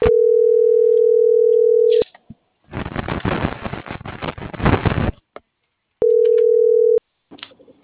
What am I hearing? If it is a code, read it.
Ringback tone (North America), 440 + 480 Hz.